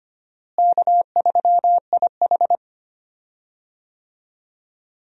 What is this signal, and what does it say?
Morse code, heard as a keyed tone, about 25 wpm, 700 Hz. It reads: K3IH
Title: USA Call Signs - 4 Characters - Repeated 3x - 25wpm